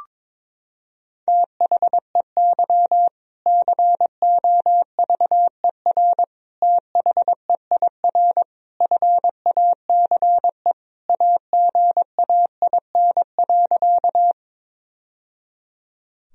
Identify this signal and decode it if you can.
Morse code, heard as a keyed tone, about 22 wpm, 700 Hz. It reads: THEY COVER THEIR FACE AGAIN.
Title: Sentences from Top 1000 Words - Review Part 1 - 22wpm